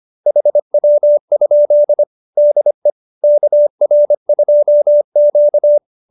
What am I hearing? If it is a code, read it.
Morse code, heard as a keyed tone, about 25 wpm, 590 Hz. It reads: HW? DE KR2Q
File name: QSO220 25